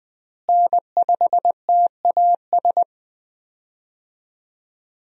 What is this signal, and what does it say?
Morse code, heard as a keyed tone, about 20 wpm, 700 Hz. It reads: N5TAS
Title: USA Call Signs - 5 Characters - Repeated 3x - 20wpm